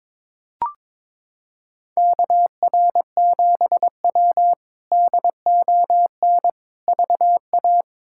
Morse code, heard as a keyed tone, about 22 wpm, 700 Hz. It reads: KR7W DON VA
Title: K1USN SST QSO Practice - 22wpm